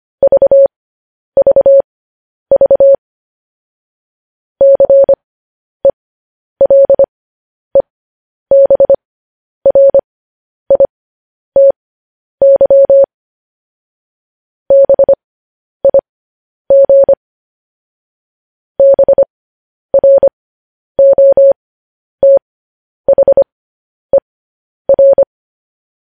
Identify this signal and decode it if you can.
Morse code, heard as a keyed tone, about 25 wpm, spaced as for 10 wpm, 550 Hz. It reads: VVV CELEBRITY BIG BROTHER
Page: News Headlines in Morse Code at 10 WPM